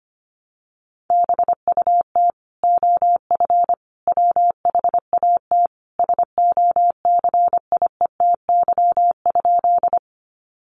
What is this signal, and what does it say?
Morse code, heard as a keyed tone, about 25 wpm, 700 Hz. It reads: BUT OF WHAT SOCIETY?
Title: Sentences from Top 700 Words 25wpm